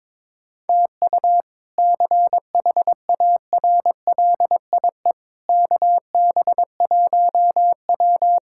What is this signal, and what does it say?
Morse code, heard as a keyed tone, about 22 wpm, 700 Hz. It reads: TU CHARLIE KB1W
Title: K1USN SST QSO Practice - 22wpm